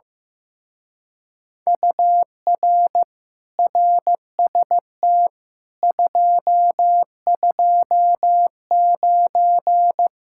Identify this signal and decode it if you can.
Morse code, heard as a keyed tone, about 15 wpm, 700 Hz. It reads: UR RST 229
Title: Speed-Racing Common QSO Phrases Encore - 15wpm